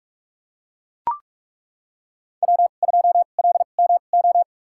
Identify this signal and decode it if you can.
Morse code, heard as a keyed tone, about 43 wpm, spaced as for 32 wpm, 700 Hz. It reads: W2LMO
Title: USA Call Signs - 5 Characters - 2x Character Spacing - 45wpm